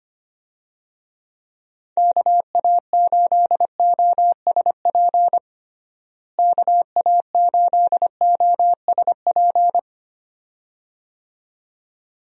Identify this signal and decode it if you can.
Morse code, heard as a keyed tone, about 25 wpm, 700 Hz. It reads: KA8OSP KA8OSP
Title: USA Call Signs - 6 Characters - Repeated 2x - 25wpm